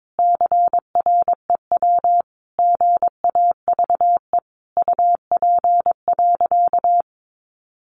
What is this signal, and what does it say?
Morse code, heard as a keyed tone, about 22 wpm, 700 Hz. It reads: CREW GAVE UP.
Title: Sentences from Top 1500 Words 22wpm